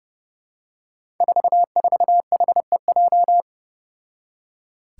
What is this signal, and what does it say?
Morse code, heard as a keyed tone, about 30 wpm, 700 Hz. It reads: 44HEJ
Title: Five Challenging Letter-Number Combinations 30wpm